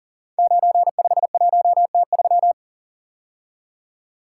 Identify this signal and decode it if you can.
Morse code, heard as a keyed tone, about 38 wpm, 700 Hz. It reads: 951T3
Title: Code Groups - 5 Letter:Numbers - 40wpm